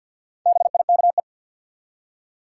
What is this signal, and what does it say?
Morse code, heard as a keyed tone, about 47 wpm, 700 Hz. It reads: BIKE
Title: Four Letter Words - 50wpm